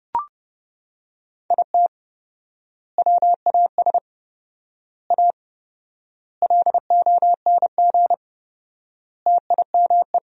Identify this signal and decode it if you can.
Morse code, heard as a keyed tone, about 32 wpm, 700 Hz. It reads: IT WAS A LONG TIME
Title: 4x Word-Spacing - Sentences from Top 500 Words - 30wpm